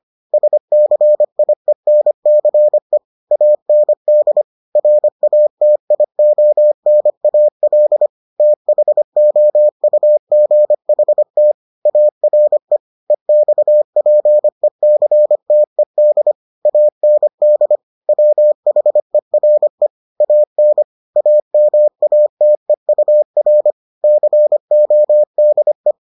Morse code, heard as a keyed tone, about 25 wpm, 600 Hz. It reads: SCIENCE AND RATIONAL THOUGHT ARE EXPECTED AND WHERE AN AMATEUR CODE